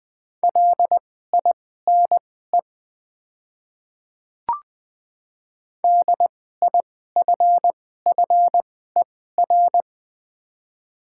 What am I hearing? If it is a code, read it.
Morse code, heard as a keyed tone, about 20 wpm, spaced as for 14 wpm, 700 Hz. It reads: LINE DIFFER
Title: 2x Character Spacing - Top 200 Words - 20wpm